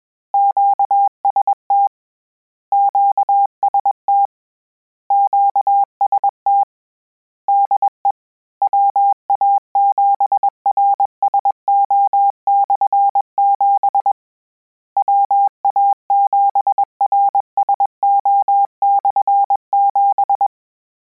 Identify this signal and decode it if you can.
Morse code, heard as a keyed tone, about 21 wpm, 800 Hz. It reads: QST QST QST DE WA7RSO/7 WA7RSO/7
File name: Combinations / Combinations-mp3s / Combinations@20